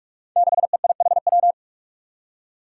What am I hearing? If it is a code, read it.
Morse code, heard as a keyed tone, about 47 wpm, 700 Hz. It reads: 6EIHW
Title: Five Challenging Letter-Number Combinations 45wpm